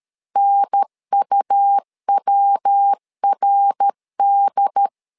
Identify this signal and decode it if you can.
Morse code, heard as a keyed tone, about 13 wpm, 800 Hz. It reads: NUWRD
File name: LEZIONI / 3 MWRO / 3 MWRO 12wpm 1730